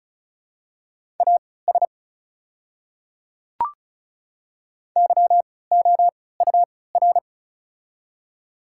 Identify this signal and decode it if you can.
Morse code, heard as a keyed tone, about 36 wpm, spaced as for 20 wpm, 700 Hz. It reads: AS YOUR
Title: 3x Character Spacing - Top 100 Words - 35wpm